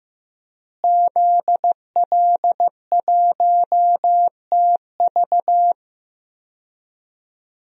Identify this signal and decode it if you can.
Morse code, heard as a keyed tone, about 16 wpm, 700 Hz. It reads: ZL1TV
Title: DXCC-100 Call Signs - 5 Characters - Repeated 3x - 15wpm